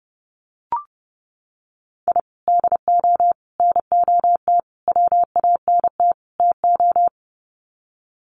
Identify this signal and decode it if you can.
Morse code, heard as a keyed tone, about 30 wpm, 700 Hz. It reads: I DO NOT WANT TO
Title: Sets of 5 Words Encore2 30wpm